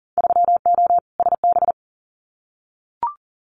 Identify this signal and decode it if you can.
Morse code, heard as a keyed tone, about 38 wpm, 700 Hz. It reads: 3O SB
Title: Winter Field Day Exchanges Encore 40wpm